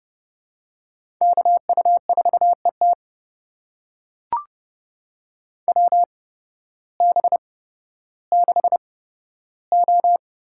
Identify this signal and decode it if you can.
Morse code, heard as a keyed tone, about 32 wpm, spaced as for 14 wpm, 700 Hz. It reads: KU4ET WB6O
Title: USA Call Signs - 8x Character Spacing - 30wpm